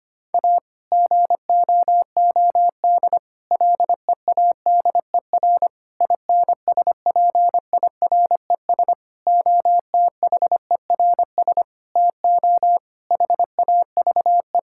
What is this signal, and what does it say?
Morse code, heard as a keyed tone, about 25 wpm, 700 Hz. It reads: A GOOD LEADER INSPIRES OTHERS TO HAVE
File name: practice / 25wpm / Difficult Sentences with Hints - 25wpm